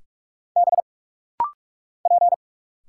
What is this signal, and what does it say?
Morse code, heard as a keyed tone, about 44 wpm, 700 Hz. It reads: B P
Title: Single Letters - Rapid-Fire 45wpm